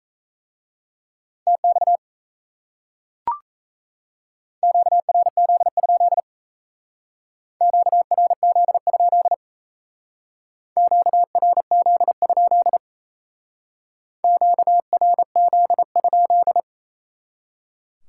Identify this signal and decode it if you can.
Morse code, heard as a keyed tone, about 37 wpm, 700 Hz. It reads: TX QRZ? QRZ? QRZ? QRZ?